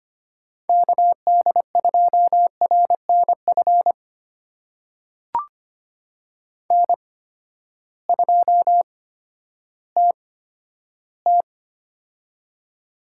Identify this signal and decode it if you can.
Morse code, heard as a keyed tone, about 25 wpm, spaced as for 12 wpm, 700 Hz. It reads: KD2RNF N2TT
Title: USA Call Signs - 8x Character Spacing - 25wpm